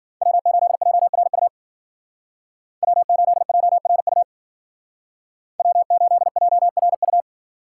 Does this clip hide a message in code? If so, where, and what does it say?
Yes: Morse code from 0.2 to 7.2 s, heard as a keyed tone, about 52 wpm, 700 Hz. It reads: W8JRU W8JRU W8JRU